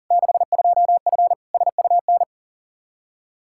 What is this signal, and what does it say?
Morse code, heard as a keyed tone, about 40 wpm, 700 Hz. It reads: 62F SUN
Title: Practice QSO - Exchange 2 - Info - 40wpm